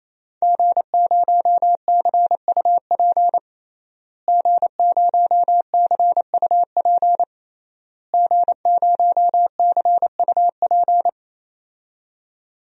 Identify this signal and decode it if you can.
Morse code, heard as a keyed tone, about 28 wpm, 700 Hz. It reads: G0CUP G0CUP G0CUP
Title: DXCC-100 Call Signs - 5 Characters - Repeated 3x - 28wpm